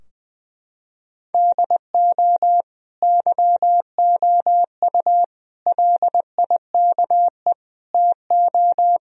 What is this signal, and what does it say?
Morse code, heard as a keyed tone, about 20 wpm, 700 Hz. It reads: DO YOU LIKE TO
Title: Sentences from Top 1000 Words - Review Part 2 - 20wpm